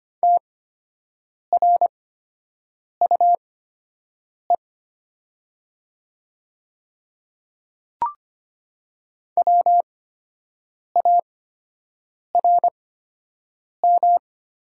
Morse code, heard as a keyed tone, about 25 wpm, spaced as for 7 wpm, 700 Hz. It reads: TRUE WARM
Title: 8x Character Spacing - Top 500 Words - 25wpm